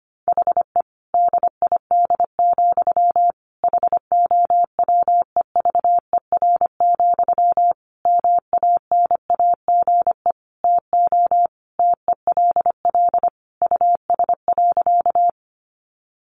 Morse code, heard as a keyed tone, about 25 wpm, 700 Hz. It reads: HE DID, HOWEVER, MANAGE TO TELL US.
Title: Sentences from Top 1400 Words 25wpm